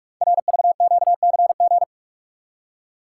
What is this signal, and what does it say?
Morse code, heard as a keyed tone, about 48 wpm, 700 Hz. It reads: RVQCG
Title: Code Groups - 5 Letters - 45wpm